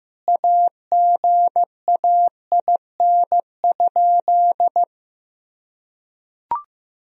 Morse code, heard as a keyed tone, about 15 wpm, 700 Hz. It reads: AGAIN?
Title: Sentences from Top 1000 Words - Review Part 1 - 15wpm